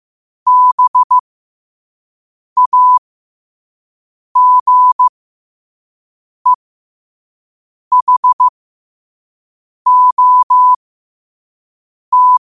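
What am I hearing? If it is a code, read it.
Morse code, heard as a keyed tone, about 15 wpm, spaced as for 5 wpm, 1000 Hz. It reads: BAGEHOT
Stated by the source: Morse at 5 wpm.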